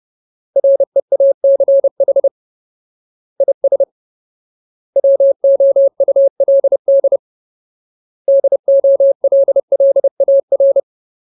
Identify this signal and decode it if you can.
Morse code, heard as a keyed tone, about 30 wpm, 550 Hz. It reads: REACH IS WOULD DOLLAR
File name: Ultimate Word List / 30WPM Ultimate